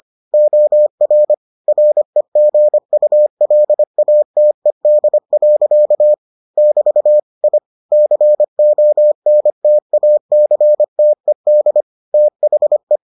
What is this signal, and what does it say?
Morse code, heard as a keyed tone, about 25 wpm, 600 Hz. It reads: OR REGULATED. <BT> I CONTACTED THE